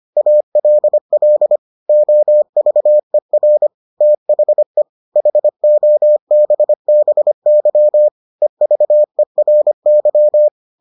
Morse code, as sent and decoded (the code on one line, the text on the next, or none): .- .-.. .-.. / --- ...- . .-. / - .... / .... --- -... -... -.-- / . ...- . .-. -.--
ALL OVER THE HOBBY EVERY